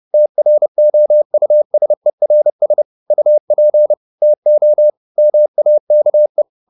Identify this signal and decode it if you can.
Morse code, heard as a keyed tone, about 30 wpm, 600 Hz. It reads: TROUSERS UP TO MAKE
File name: the old man and the sea / the old man and the sea 1 30wpm